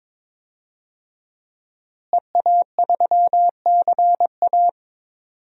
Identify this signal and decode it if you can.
Morse code, heard as a keyed tone, about 22 wpm, 700 Hz. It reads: EA3CA
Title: DXCC-100 Call Signs - 5 Characters - Repeated 3x - 22wpm